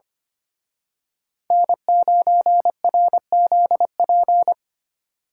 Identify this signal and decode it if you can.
Morse code, heard as a keyed tone, about 28 wpm, 700 Hz. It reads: N9RZP
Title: Speed Racing - USA 1x3 Call Signs 25wpm